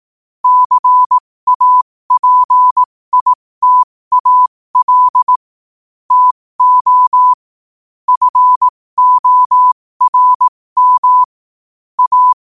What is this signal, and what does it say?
Morse code, heard as a keyed tone, about 18 wpm, spaced as for 15 wpm, 1000 Hz. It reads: CAPITAL TO FORM A
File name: qotdfiles / 15wpm